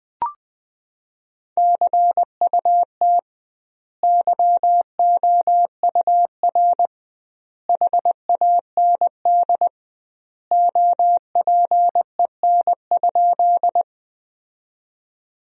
Morse code, heard as a keyed tone, about 20 wpm, 700 Hz. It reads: CUT YOUR HAND OPEN?
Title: 2x Word-Spacing - Sentences from Top 300 Words - 20wpm